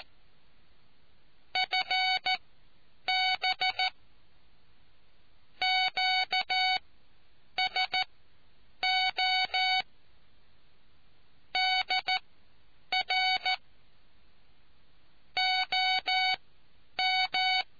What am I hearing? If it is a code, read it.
Morse code, heard as a keyed tone, about 14 wpm, spaced as for 8 wpm, 2250 Hz. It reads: FB QSO DR OM